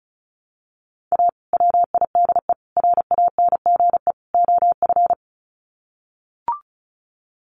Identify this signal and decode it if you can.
Morse code, heard as a keyed tone, about 34 wpm, 700 Hz. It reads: A WIDE RANGE OF
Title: Sets of 4 Words 35wpm